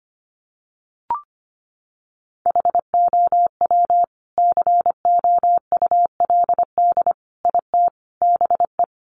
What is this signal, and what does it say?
Morse code, heard as a keyed tone, about 25 wpm, 700 Hz. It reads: HOW COULD IT BE
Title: Sentences from Top 1200 Words 25wpm